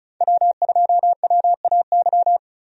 Morse code, heard as a keyed tone, about 34 wpm, 700 Hz. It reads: W2WAY
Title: USA Call Signs - 5 Characters - 35wpm